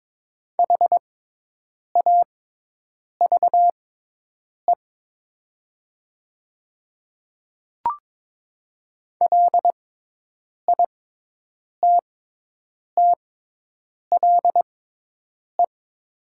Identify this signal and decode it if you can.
Morse code, heard as a keyed tone, about 22 wpm, spaced as for 8 wpm, 700 Hz. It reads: HAVE LITTLE
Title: 6x Character Spacing - Top 200 Words - 22wpm